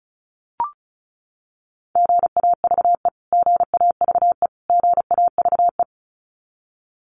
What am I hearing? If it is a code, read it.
Morse code, heard as a keyed tone, about 34 wpm, 700 Hz. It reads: GAVE GAVE GAVE
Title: Top 500 Words Repeated 35wpm